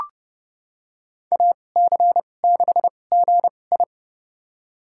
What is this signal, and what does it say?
Morse code, heard as a keyed tone, about 30 wpm, spaced as for 22 wpm, 700 Hz. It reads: AC6GI
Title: USA Call Signs - 5 Characters - 2x Character Spacing - 30wpm